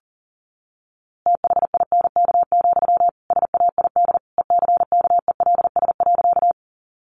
Morse code, heard as a keyed tone, about 37 wpm, 700 Hz. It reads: THINK, SAID ECKELS.
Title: A Sound Of Thunder 40wpm